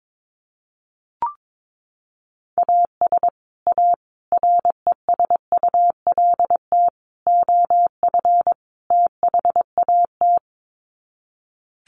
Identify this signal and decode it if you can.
Morse code, heard as a keyed tone, about 22 wpm, 700 Hz. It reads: AS A RESULT OF THAT
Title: Sets of 5 Words Encore 22wpm